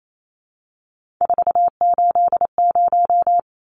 Morse code, heard as a keyed tone, about 28 wpm, 700 Hz. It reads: 480
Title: Numbers - 3-Digit 28wpm